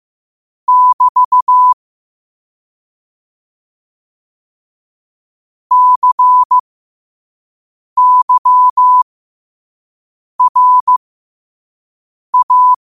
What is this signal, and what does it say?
Morse code, heard as a keyed tone, about 15 wpm, spaced as for 5 wpm, 1000 Hz. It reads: <BT> CYRA
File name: qotdfiles / 5wpm